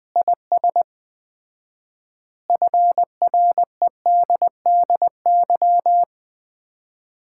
Morse code, heard as a keyed tone, about 20 wpm, 700 Hz. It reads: IS FREDDY
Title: Common QSO Phrases - 4x Word-Spacing - 20wpm